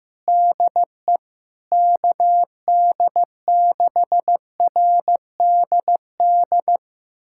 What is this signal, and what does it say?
Morse code, heard as a keyed tone, about 15 wpm, 700 Hz. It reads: DE KD6RDD
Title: USA Callsign DE Callsign - 6 Characters - 15wpm